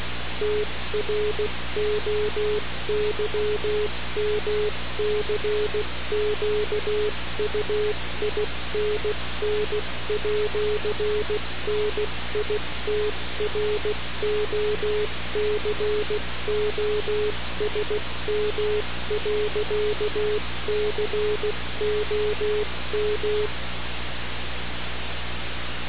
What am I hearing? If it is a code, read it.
Morse code, heard as a keyed tone, about 16 wpm, 420 Hz, with noise. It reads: TROYMCQUINN@NITROCOSM.COM
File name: radiocap / REC z